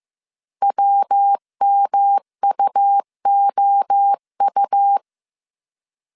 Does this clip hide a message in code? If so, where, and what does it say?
Yes: Morse code from 0.6 to 5.0 s, heard as a keyed tone, about 15 wpm, 800 Hz. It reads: WMUOU